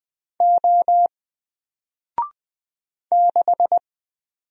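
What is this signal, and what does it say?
Morse code, heard as a keyed tone, about 20 wpm, 700 Hz. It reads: O 6
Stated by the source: Single letter and number Morse drill at 20 wpm, from N4LCD.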